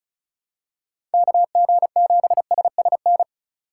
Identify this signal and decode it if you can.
Morse code, heard as a keyed tone, about 34 wpm, 700 Hz. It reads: KG7SSN
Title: USA Call Signs - 6 Characters - 2x Character Spacing - 35wpm